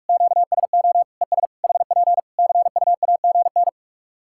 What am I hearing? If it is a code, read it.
Morse code, heard as a keyed tone, about 46 wpm, 700 Hz. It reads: QSO ES HP CUAGN
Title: K1USN SST QSO Practice 45wpm